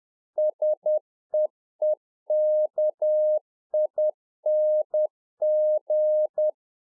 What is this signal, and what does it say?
Morse code, heard as a keyed tone, about 10 wpm, 610 Hz. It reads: SEEKING